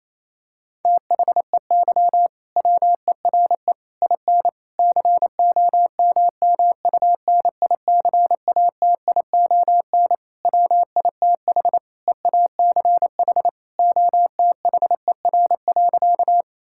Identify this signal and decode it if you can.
Morse code, heard as a keyed tone, about 28 wpm, 700 Hz. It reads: THEY WERE IN COMMUNICATION WITH EACH OTHER.
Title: Sentences from Top 1000 Words - Review Part 6 - 28wpm